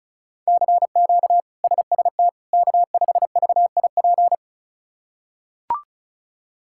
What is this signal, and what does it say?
Morse code, heard as a keyed tone, about 34 wpm, 700 Hz. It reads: CQ SST K5VIP
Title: K1USN SST QSO Practice - 35wpm